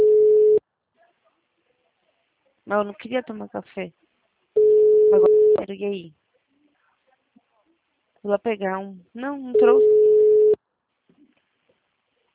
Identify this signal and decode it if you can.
Ringback tone (Europe, 425 Hz), with speech.